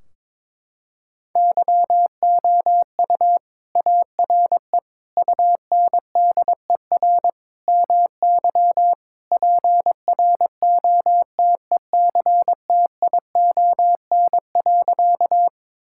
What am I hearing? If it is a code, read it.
Morse code, heard as a keyed tone, about 21 wpm, 700 Hz. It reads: YOU ARE UNDER MY PROTECTION.
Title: Sentences from Top 1600 Words 22wpm